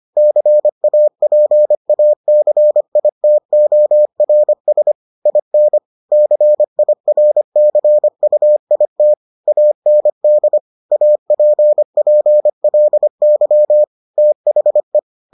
Morse code, heard as a keyed tone, about 25 wpm, 600 Hz. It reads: CAPACITORS IN CIRCUIT AND APPLY THE